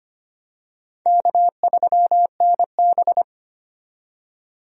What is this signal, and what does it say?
Morse code, heard as a keyed tone, about 25 wpm, 700 Hz. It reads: K3NB
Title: USA Call Signs - 4 Characters - 25wpm